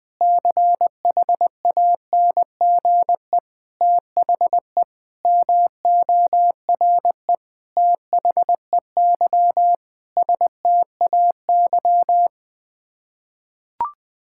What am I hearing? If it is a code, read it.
Morse code, heard as a keyed tone, about 20 wpm, 700 Hz. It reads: CHANGE THE MORE THEY STAY
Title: Sets of 8 Words - 20wpm